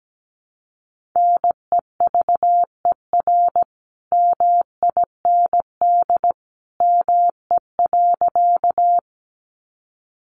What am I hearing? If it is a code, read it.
Morse code, heard as a keyed tone, about 17 wpm, 700 Hz. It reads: NEVER MIND ME.
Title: Sentences from Top 500 Words 17wpm